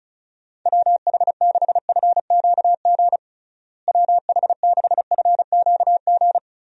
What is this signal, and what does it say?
Morse code, heard as a keyed tone, about 36 wpm, 700 Hz. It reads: WH6FQG WH6FQG
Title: USA Call Signs - 6 Characters - Repeated 3x - 35wpm